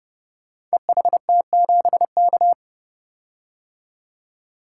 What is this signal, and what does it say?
Morse code, heard as a keyed tone, about 31 wpm, 700 Hz. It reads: EHT7K
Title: Code Groups - 5 Letter:Numbers - 30wpm